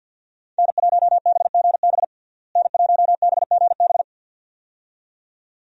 Morse code, heard as a keyed tone, about 46 wpm, 700 Hz. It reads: N1BGB N1BGB
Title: USA Call Signs - 5 Characters - Repeated 2x - 50wpm